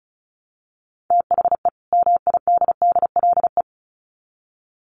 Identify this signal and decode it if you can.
Morse code, heard as a keyed tone, about 34 wpm, 700 Hz. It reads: THE MIDDLE
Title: Sets of 2 Words 35wpm